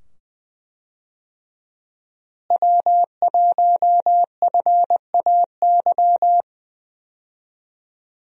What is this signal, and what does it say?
Morse code, heard as a keyed tone, about 20 wpm, 700 Hz. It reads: W1FAY
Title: USA Call Signs - 5 Characters - Repeated 2x - 20wpm